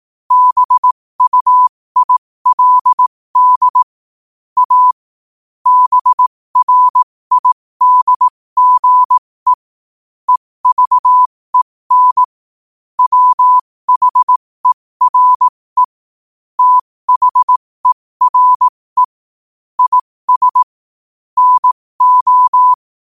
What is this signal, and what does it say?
Morse code, heard as a keyed tone, about 18 wpm, spaced as for 15 wpm, 1000 Hz. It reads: BUILD A BRIDGE EVEN WHERE THERE IS NO
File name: qotdfiles / 15wpm